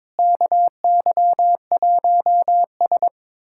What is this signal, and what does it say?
Morse code, heard as a keyed tone, about 22 wpm, 700 Hz. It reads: KY1S
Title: USA Callsign DE Callsign - 4 Characters - 22wpm